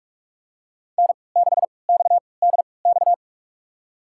Morse code, heard as a keyed tone, about 44 wpm, spaced as for 25 wpm, 700 Hz. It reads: N6XDX
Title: USA Call Signs - 5 Characters - 3x Character Spacing - 45wpm